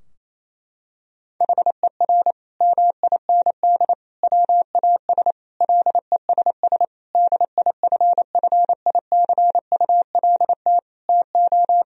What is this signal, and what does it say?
Morse code, heard as a keyed tone, about 28 wpm, 700 Hz. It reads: HER MIND WAS LESS DIFFICULT TO
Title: Sentences from Top 1000 Words - Review Part 2 - 28wpm